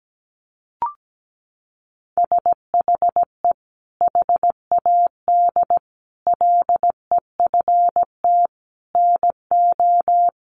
Morse code, heard as a keyed tone, about 17 wpm, 700 Hz. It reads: SHE HAD LEFT NO
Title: Sentences from Top 1100 Words 17wpm